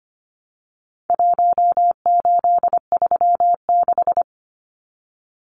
Morse code, heard as a keyed tone, about 25 wpm, 700 Hz. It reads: 1836
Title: Numbers - Contest Style - 4-Digit 25wpm